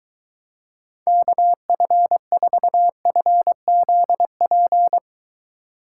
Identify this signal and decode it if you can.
Morse code, heard as a keyed tone, about 23 wpm, 700 Hz. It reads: KF4FZP